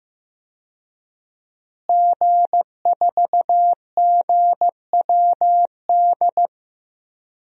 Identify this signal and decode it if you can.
Morse code, heard as a keyed tone, about 15 wpm, 700 Hz. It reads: G4GWD
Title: DXCC-100 Call Signs - 5 Characters - Repeated 2x - 15wpm